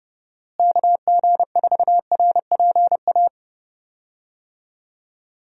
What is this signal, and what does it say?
Morse code, heard as a keyed tone, about 30 wpm, 700 Hz. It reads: KG4RPA
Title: USA Call Signs - 6 Characters - Repeated 3x - 30wpm